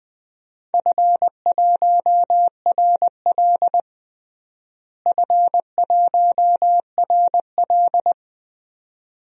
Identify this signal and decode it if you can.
Morse code, heard as a keyed tone, about 20 wpm, 700 Hz. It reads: F1RL F1RL
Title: DXCC-100 Call Signs - 4 Characters - Repeated 2x - 20wpm